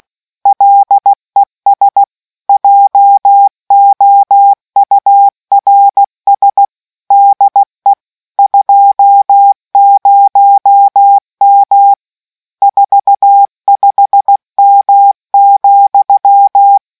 Morse code, heard as a keyed tone, about 16 wpm, 800 Hz. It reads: LES JOURS DE 20M 45M,